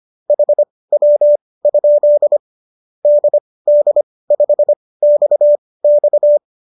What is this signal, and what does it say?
Morse code, heard as a keyed tone, about 25 wpm, spaced as for 18 wpm, 590 Hz. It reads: HW? DD5XX
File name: qso218 18